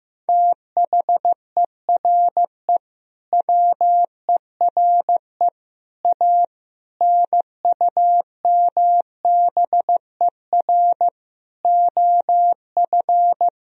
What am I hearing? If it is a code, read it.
Morse code, heard as a keyed tone, about 15 wpm, 700 Hz. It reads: THERE WERE A NUMBER OF
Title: Sentences from Top 1000 Words - Review Part 1 - 15wpm